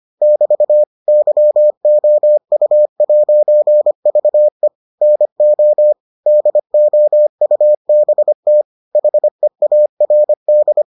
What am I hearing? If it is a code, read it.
Morse code, heard as a keyed tone, about 25 wpm, 600 Hz. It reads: <BT> YOU'VE NO DOUBT HEARD